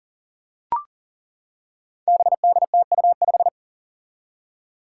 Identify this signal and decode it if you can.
Morse code, heard as a keyed tone, about 40 wpm, 700 Hz. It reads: BDTU5
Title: Five Challenging Letter-Number Combinations 40wpm